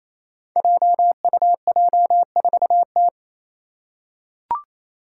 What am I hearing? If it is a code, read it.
Morse code, heard as a keyed tone, about 28 wpm, 700 Hz. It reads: JUJ4T
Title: Five Challenging Letter-Number Combinations 28wpm